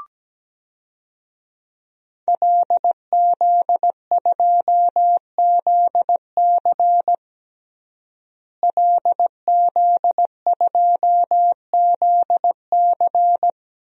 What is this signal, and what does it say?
Morse code, heard as a keyed tone, about 17 wpm, 700 Hz. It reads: LZ2ZC LZ2ZC
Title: DXCC-100 Call Signs - 5 Characters - Repeated 2x - 17wpm